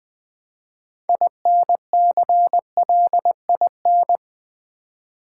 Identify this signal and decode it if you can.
Morse code, heard as a keyed tone, about 20 wpm, 700 Hz. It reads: INCLIN
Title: Build A Word - 2 Letters At A Time - 8-Letter Words - 20wpm